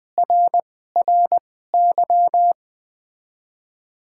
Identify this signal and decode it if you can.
Morse code, heard as a keyed tone, about 20 wpm, spaced as for 15 wpm, 700 Hz. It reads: RRY
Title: 2x Character Spacing - Top 300 Words - 20wpm